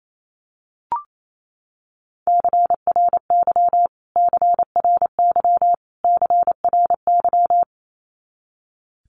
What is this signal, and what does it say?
Morse code, heard as a keyed tone, about 28 wpm, 700 Hz. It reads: CRY CRY CRY
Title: Top 500 Words Repeated 28wpm